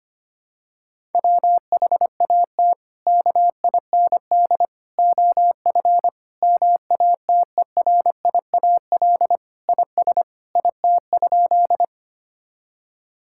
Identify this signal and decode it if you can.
Morse code, heard as a keyed tone, about 25 wpm, 700 Hz. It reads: WHAT KIND OF MATERIAL IS IT?
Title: Sentences from Top 1000 Words - Review Part 5 - 25wpm